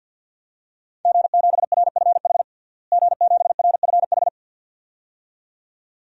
Morse code, heard as a keyed tone, about 51 wpm, 700 Hz. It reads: G7RFH G7RFH
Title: DXCC-100 Call Signs - 5 Characters - Repeated 2x - 50wpm